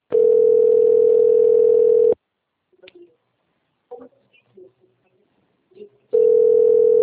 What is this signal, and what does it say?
Ringback tone (North America), 440 + 480 Hz, with speech.
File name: rg-1600-unknown- - - .27342